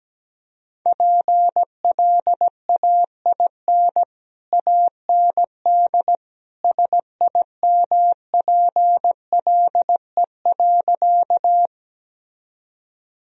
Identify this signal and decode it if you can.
Morse code, heard as a keyed tone, about 17 wpm, 700 Hz. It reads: PLAIN AND SIMPLE.
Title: Sentences from Top 1000 Words - Review Part 2 - 17wpm